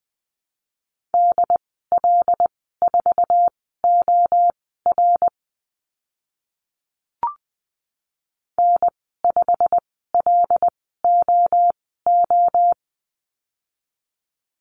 Morse code, heard as a keyed tone, about 19 wpm, spaced as for 14 wpm, 700 Hz. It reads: DL4OR N5LOO
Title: 088 Lesson 15g - Number 5 - Call Signs 20wpm